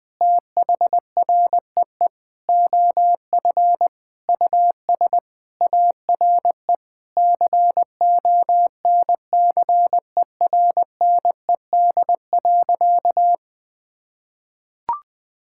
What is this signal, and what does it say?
Morse code, heard as a keyed tone, about 20 wpm, 700 Hz. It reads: THREE OF US ARE CONCERNED.
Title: Sentences from Top 1000 Words - Review Part 6 - 20wpm